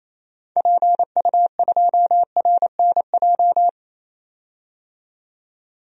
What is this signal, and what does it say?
Morse code, heard as a keyed tone, about 28 wpm, 700 Hz. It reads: PU2RNJ
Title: DXCC-100 Call Signs - 6 Characters - 28wpm